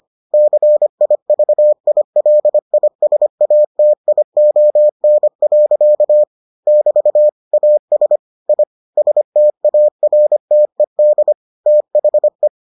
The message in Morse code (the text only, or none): CIVILISATION. <BT> AS I STARTED THE